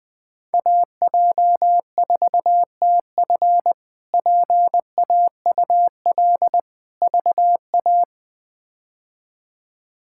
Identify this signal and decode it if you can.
Morse code, heard as a keyed tone, about 20 wpm, 700 Hz. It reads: AJ4TF PAUL VA
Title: K1USN SST QSO Practice - 20wpm